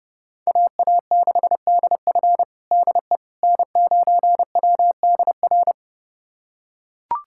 Morse code, heard as a keyed tone, about 32 wpm, 700 Hz. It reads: AA6DF DE N9WDR
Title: USA Callsign DE Callsign - 5 Characters - 30wpm